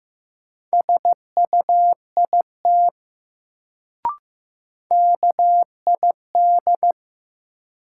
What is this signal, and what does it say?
Morse code, heard as a keyed tone, about 15 wpm, 700 Hz. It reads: SUIT KID
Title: Sentences from Top 1600 Words - New Word Drill - 15wpm